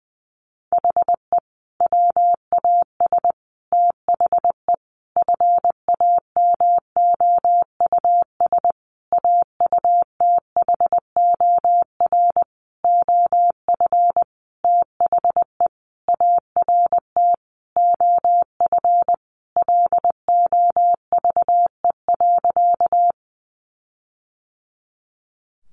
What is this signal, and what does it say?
Morse code, heard as a keyed tone, about 20 wpm, 700 Hz. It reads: HE WAS THE FAMOUS AUTHOR OF THE ART OF LOVE.
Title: Sentences from Top 1500 Words 20wpm